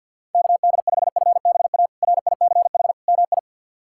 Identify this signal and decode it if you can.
Morse code, heard as a keyed tone, about 51 wpm, 690 Hz. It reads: KD5FBA RICH MI
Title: K1USN SST QSO Practice - 50wpm